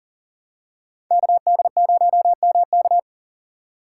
Morse code, heard as a keyed tone, about 40 wpm, 700 Hz. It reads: KD0MK
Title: USA Call Signs - 5 Characters - 6x Character Spacing - 40wpm